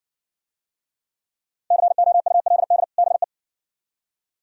Morse code, heard as a keyed tone, about 60 wpm, 690 Hz. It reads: COULD BE